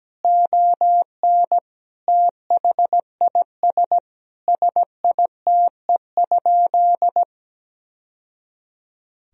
Morse code, heard as a keyed tone, about 17 wpm, 700 Hz. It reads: ON THIS SITE?
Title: Sentences from Top 1000 Words - Review Part 4 - 17wpm